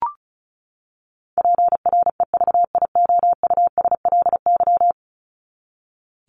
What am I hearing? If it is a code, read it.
Morse code, heard as a keyed tone, about 34 wpm, 700 Hz. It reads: PREVIOUSLY
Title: LY Suffix - Top 100 Words with Base Words 35wpm